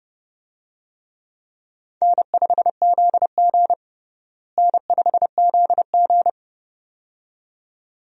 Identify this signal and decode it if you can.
Morse code, heard as a keyed tone, about 32 wpm, 700 Hz. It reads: N5ZG N5ZG
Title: USA Call Signs - 4 Characters - Repeated 2x - 30wpm